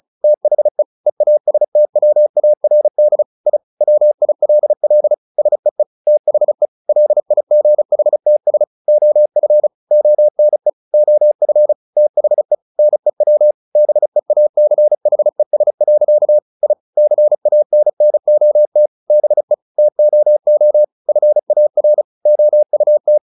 Morse code, heard as a keyed tone, about 34 wpm, 600 Hz. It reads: THE EASTWARD I WILL SEE THE LIGHTS OF ONE OF THE NEW BEACHES. I CANNOT BE TOO FAR OUT
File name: the old man and the sea 2 35wpm